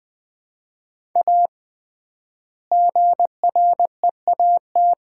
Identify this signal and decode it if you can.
Morse code, heard as a keyed tone, about 21 wpm, 700 Hz. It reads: A GREAT
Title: 3x Word-Spacing - Sets of 2 Words - 20wpm